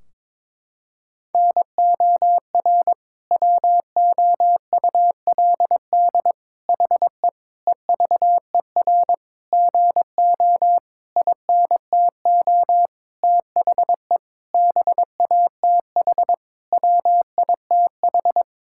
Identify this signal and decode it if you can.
Morse code, heard as a keyed tone, about 22 wpm, 700 Hz. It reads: NOR WOULD HE EVER GO INTO THE BATH WITH
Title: Sentences from Top 1600 Words 22wpm